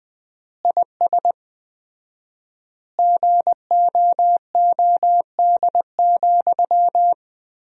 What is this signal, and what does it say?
Morse code, heard as a keyed tone, about 20 wpm, 700 Hz. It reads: IS GOOD,
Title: 4x Word-Spacing - Sentences from Top 200 Words - 20wpm